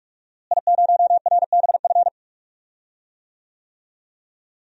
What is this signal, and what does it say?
Morse code, heard as a keyed tone, about 46 wpm, 700 Hz. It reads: I0RBF
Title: DXCC-100 Call Signs - 5 Characters - Repeated 2x - 45wpm